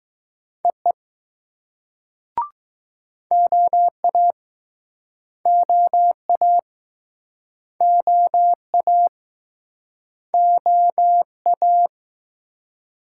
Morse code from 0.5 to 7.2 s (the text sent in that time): EE OA OA